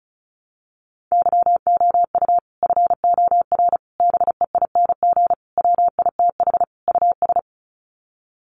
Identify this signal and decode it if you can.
Morse code, heard as a keyed tone, about 34 wpm, 700 Hz. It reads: YOU FOR BEING WITH US
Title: Sets of 5 Words 35wpm